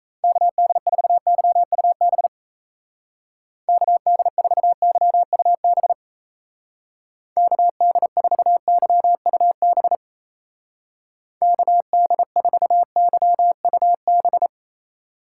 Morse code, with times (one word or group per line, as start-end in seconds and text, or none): 0.2-2.3 KD4YUB
3.7-5.9 KD4YUB
7.4-9.9 KD4YUB
11.4-14.5 KD4YUB